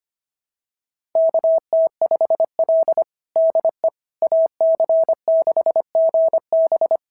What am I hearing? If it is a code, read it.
Morse code, heard as a keyed tone, about 25 wpm, 650 Hz. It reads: KT5L DE AC6GB
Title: SOTA QSO Practice 25wpm